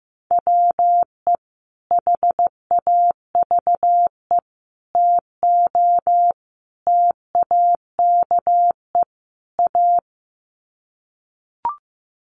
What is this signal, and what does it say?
Morse code, heard as a keyed tone, about 14 wpm, 700 Hz. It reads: WE HAVE TO TAKE A
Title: Sets of 5 Words Encore 15wpm